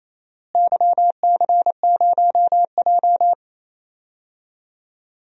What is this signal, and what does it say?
Morse code, heard as a keyed tone, about 28 wpm, 700 Hz. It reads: YC0J
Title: DXCC-100 Call Signs - 4 Characters - Repeated 3x - 28wpm